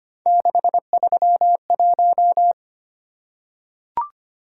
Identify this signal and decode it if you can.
Morse code, heard as a keyed tone, about 25 wpm, 700 Hz. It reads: 631
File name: Three Letter-Number Combinations - 25wpm